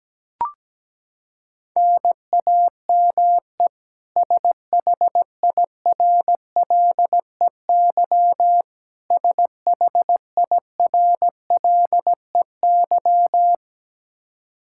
Morse code, heard as a keyed tone, about 17 wpm, 700 Hz. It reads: NAME SHIRLEY SHIRLEY
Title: K1USN SST QSO Practice 17wpm